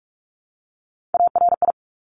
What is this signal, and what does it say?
Morse code, heard as a keyed tone, about 44 wpm, 690 Hz. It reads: ARI
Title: Top Three Letter Combinations in Words 45wpm